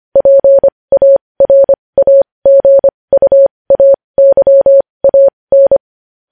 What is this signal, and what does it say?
Morse code, heard as a keyed tone, about 25 wpm, spaced as for 20 wpm, 550 Hz. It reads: PARAGUAYAN
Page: news-headlines-in-morse-code-at-20-wpm